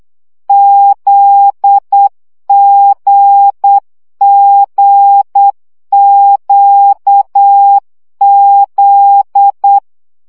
Morse code, heard as a keyed tone, about 8 wpm, 800 Hz. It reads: ZGGQZ
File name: CW gyakorlas g z q 7 baud